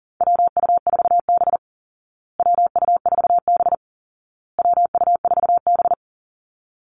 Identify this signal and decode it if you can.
Morse code, heard as a keyed tone, about 40 wpm, 700 Hz. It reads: WU4B WU4B WU4B
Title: Call Signs Repeated 2x - 40wpm